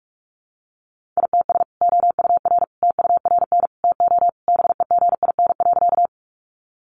Morse code, heard as a keyed tone, about 44 wpm, 700 Hz. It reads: ITS OUR TURN TO BEGIN.
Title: Sentences from Top 600 Words 45wpm